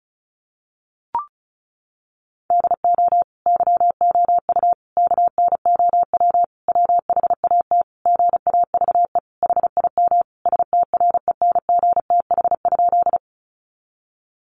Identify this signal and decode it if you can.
Morse code, heard as a keyed tone, about 34 wpm, 700 Hz. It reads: DO YOU KNOW WHAT GAVE HIM STRENGTH?
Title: Sentences from Top 1300 Words 35wpm